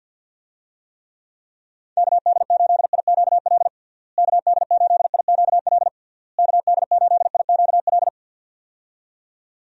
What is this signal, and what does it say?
Morse code, heard as a keyed tone, about 52 wpm, 700 Hz. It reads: KD8IQL KD8IQL KD8IQL
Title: USA Call Signs - 6 Characters - Repeated 3x - 50wpm